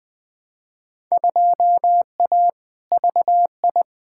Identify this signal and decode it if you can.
Morse code, heard as a keyed tone, about 20 wpm, 700 Hz. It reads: 2A VI
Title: Field Day Exchanges - 20wpm